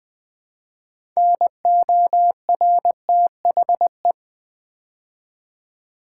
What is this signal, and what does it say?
Morse code, heard as a keyed tone, about 20 wpm, 700 Hz. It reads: NORTHE
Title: Build A Word - 2 Letters At A Time - 8-Letter Words - 20wpm